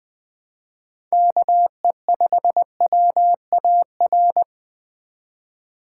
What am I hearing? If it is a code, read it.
Morse code, heard as a keyed tone, about 20 wpm, 700 Hz. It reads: KE5WAR
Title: Speed Racing - USA 2x3 Call Signs 17wpm